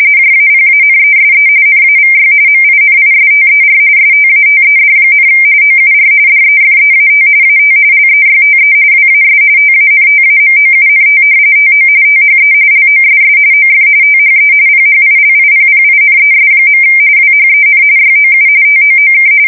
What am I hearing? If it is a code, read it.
RTTY, about 45.45 baud. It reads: JAPANESE NAVY MINISTER STATED, OUR DEFEAT AT LEYTE WAS TANTAMOUNT TO THE LOSS OF THE PHILIPPINES. WHEN YOU TOOK
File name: fsk170 / battleshipnj